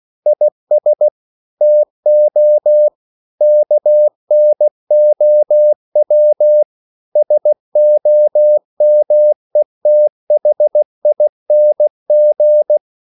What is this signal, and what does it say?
Morse code, heard as a keyed tone, about 16 wpm, 600 Hz. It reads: IS TO KNOW SOMETHING
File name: CW / Practice / EnglishQuotations0216